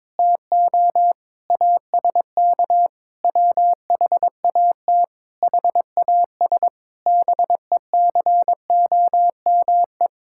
Morse code, heard as a keyed tone, about 22 wpm, 700 Hz. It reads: TO ASK WHAT HAS BECOME
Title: Sentences from Top 1000 Words - Review Part 5 - 22wpm